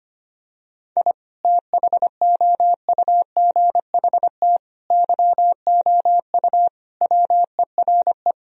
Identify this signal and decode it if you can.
Morse code, heard as a keyed tone, about 25 wpm, 700 Hz. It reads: I THOUGHT YOU WERE
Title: Sentences from Top 1000 Words - Review Part 4 - 25wpm